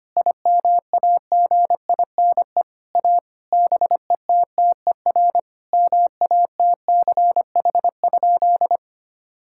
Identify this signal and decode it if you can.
Morse code, heard as a keyed tone, about 25 wpm, 700 Hz. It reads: IMAGINE A BETTER MATCH?
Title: Sentences from Top 1600 Words 25wpm